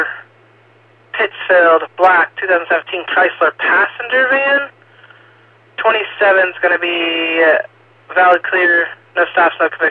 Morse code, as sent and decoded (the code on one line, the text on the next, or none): none